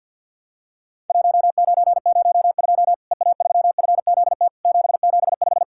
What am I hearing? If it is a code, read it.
Morse code, heard as a keyed tone, about 50 wpm, 700 Hz. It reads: 1902 EA3FZT 775
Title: Worked All Europe DX Contest - QTC Exchange with Repeats - 50wpm